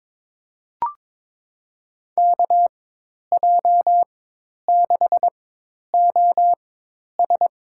Morse code, heard as a keyed tone, about 22 wpm, spaced as for 10 wpm, 700 Hz. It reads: KJ6OS
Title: USA Call Signs - 4x Character Spacing - 22wpm